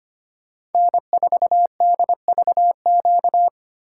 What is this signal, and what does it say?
Morse code, heard as a keyed tone, about 33 wpm, 700 Hz. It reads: N4DVQ